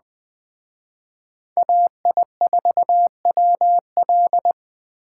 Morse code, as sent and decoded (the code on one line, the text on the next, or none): .- .. ....- .-- .-..
AI4WL